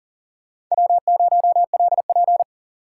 Morse code, heard as a keyed tone, about 40 wpm, 700 Hz. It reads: W0LP
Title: USA Call Signs - 8x Character Spacing - 40wpm